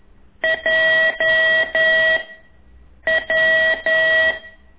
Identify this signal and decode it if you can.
Morse code, heard as a keyed tone, about 11 wpm, 1950 Hz, with noise. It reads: J W
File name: ik1weg2 / CORSI CW